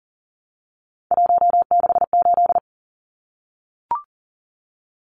Morse code, heard as a keyed tone, about 40 wpm, 700 Hz. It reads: 168
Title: Numbers - 3-Digit 40wpm